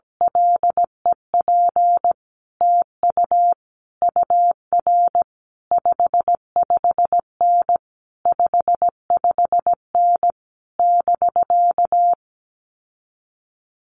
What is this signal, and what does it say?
Morse code, heard as a keyed tone, about 17 wpm, 700 Hz. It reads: LEP TU UR 55N 55N <BK>
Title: Common QSO Phrases Encore 17wpm